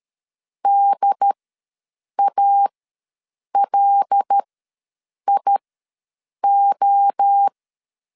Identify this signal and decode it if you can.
Morse code, heard as a keyed tone, about 13 wpm, spaced as for 7 wpm, 800 Hz. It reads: DALIO
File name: 5 LB ws 12wpm